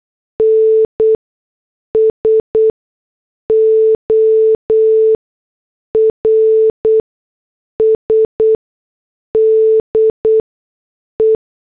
Morse code, heard as a keyed tone, about 8 wpm, spaced as for 6 wpm, 440 Hz. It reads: NSORS DE